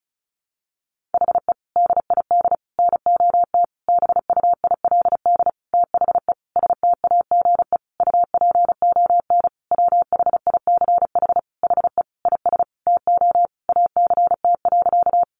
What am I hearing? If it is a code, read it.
Morse code, heard as a keyed tone, about 34 wpm, 700 Hz. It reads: HE DID NOT BUILD THE STAGE UPON WHICH HE IS TO ACT.
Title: Sentences from Top 1200 Words 35wpm